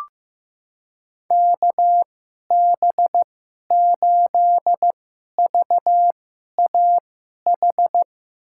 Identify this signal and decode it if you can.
Morse code, heard as a keyed tone, about 15 wpm, spaced as for 11 wpm, 700 Hz. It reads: KB8VAH
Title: USA Call Signs - 2x Character Spacing - 15wpm